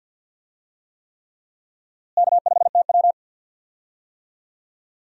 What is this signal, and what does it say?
Morse code, heard as a keyed tone, about 50 wpm, 700 Hz. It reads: K5TW
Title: USA Call Signs - 4 Characters - Repeated 3x - 50wpm